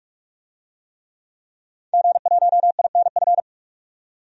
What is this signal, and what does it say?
Morse code, heard as a keyed tone, about 46 wpm, 700 Hz. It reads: G1INF
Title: DXCC-100 Call Signs - 5 Characters - 45wpm